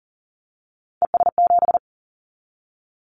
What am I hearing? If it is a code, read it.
Morse code, heard as a keyed tone, about 40 wpm, 700 Hz. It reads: ES7
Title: Callsign Prefixes for Top 50 DXCC - 40wpm